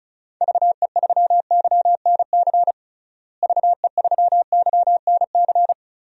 Morse code, heard as a keyed tone, about 34 wpm, 700 Hz. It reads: VE3YNC VE3YNC
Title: DXCC-100 Call Signs - 6 Characters - Repeated 2x - 35wpm